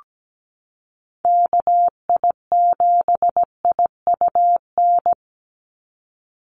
Morse code, heard as a keyed tone, about 17 wpm, 700 Hz. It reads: KI7IUN
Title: USA Call Signs 17wpm